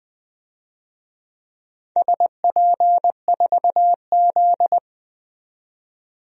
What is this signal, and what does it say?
Morse code, heard as a keyed tone, about 20 wpm, 700 Hz. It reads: SP4Z
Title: DXCC-100 Call Signs - 4 Characters - 20wpm